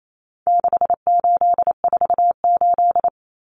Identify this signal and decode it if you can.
Morse code, heard as a keyed tone, about 28 wpm, 700 Hz. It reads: 6848
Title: Numbers - 4-Digit 28wpm